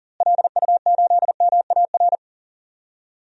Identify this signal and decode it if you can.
Morse code, heard as a keyed tone, about 40 wpm, 700 Hz. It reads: LU8MAR
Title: DXCC-100 Call Signs - 6 Characters - Repeated 2x - 40wpm